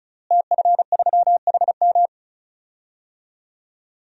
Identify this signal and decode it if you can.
Morse code, heard as a keyed tone, about 34 wpm, 700 Hz. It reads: TF3HM
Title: DXCC-100 Call Signs - 5 Characters - Repeated 2x - 35wpm